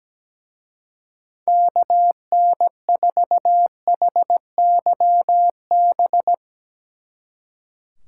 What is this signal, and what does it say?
Morse code, heard as a keyed tone, about 18 wpm, 700 Hz. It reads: KN4HYB
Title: Speed Racing - USA 2x3 Call Signs 17wpm